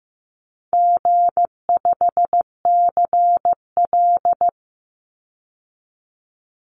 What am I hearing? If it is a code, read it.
Morse code, heard as a keyed tone, about 15 wpm, 700 Hz. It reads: G5CL
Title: Easy Call Signs - 15wpm